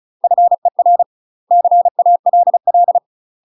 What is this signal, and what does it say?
Morse code, heard as a keyed tone, about 34 wpm, 700 Hz. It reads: FER CALL
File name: UK MASTER QSOs TRAINING / 35WPM / UK MASTER QSOs 0042